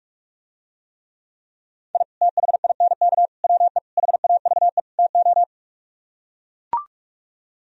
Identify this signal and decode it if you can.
Morse code, heard as a keyed tone, about 44 wpm, 700 Hz. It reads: I THINK WE HAVE TO